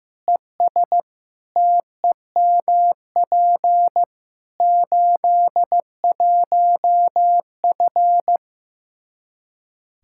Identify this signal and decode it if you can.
Morse code, heard as a keyed tone, about 15 wpm, 700 Hz. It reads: ES TEMP 81F
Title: Practice QSO - Exchange 2 - Info - 15wpm